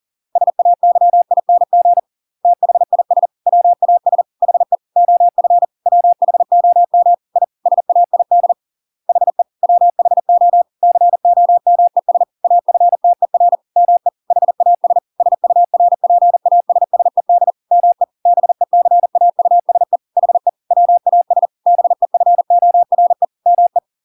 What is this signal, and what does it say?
Morse code, heard as a keyed tone, about 40 wpm, 700 Hz. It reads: SAYING THIS WAS HE OF WHOM I SAID HE WHO COMES AFTER ME HAS SURPASSED ME BECAUSE HE WAS BEFORE ME